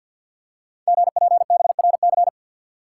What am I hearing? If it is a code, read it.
Morse code, heard as a keyed tone, about 46 wpm, 700 Hz. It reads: GPBRC
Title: Code Groups - 5 Letters - 50wpm